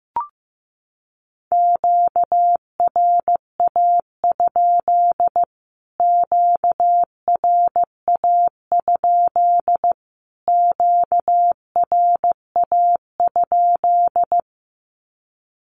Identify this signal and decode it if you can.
Morse code, heard as a keyed tone, about 15 wpm, 700 Hz. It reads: QRA? QRA? QRA?
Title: Q-Codes Repeated 3x 15wpm